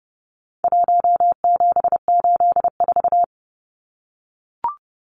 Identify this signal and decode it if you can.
Morse code, heard as a keyed tone, about 30 wpm, 700 Hz. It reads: 1784
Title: Numbers - 4-Digit 30wpm